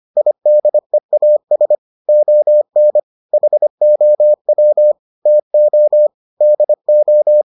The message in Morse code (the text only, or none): IDEAS ON HOW TO DO